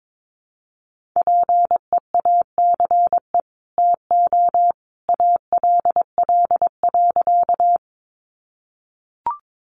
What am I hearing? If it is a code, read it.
Morse code, heard as a keyed tone, about 22 wpm, 700 Hz. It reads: PEACE TO ALL.
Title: Sentences from Top 1500 Words 22wpm